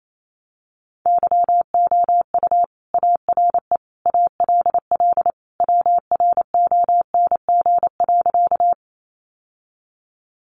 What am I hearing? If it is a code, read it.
Morse code, heard as a keyed tone, about 28 wpm, 700 Hz. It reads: YOU ARE ALL WRONG.
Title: Sentences from Top 900 Words 28wpm